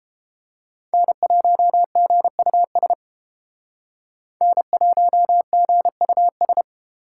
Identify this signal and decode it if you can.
Morse code, heard as a keyed tone, about 29 wpm, 700 Hz. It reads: N1GUS N1GUS